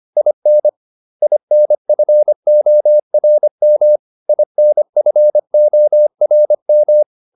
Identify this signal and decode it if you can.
Morse code, heard as a keyed tone, about 25 wpm, 600 Hz. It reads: IN INFORM INFORM